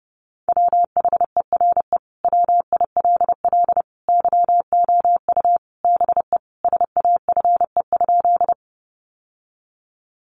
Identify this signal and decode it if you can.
Morse code, heard as a keyed tone, about 30 wpm, 700 Hz. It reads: WHERE WILL YOU BE SAFE?
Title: Sentences from Top 900 Words 30wpm